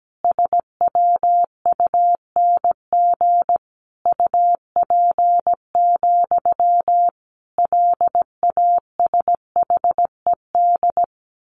Morse code, heard as a keyed tone, about 17 wpm, 700 Hz. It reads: SWUNG UP, LASHED
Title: A Sound Of Thunder 17wpm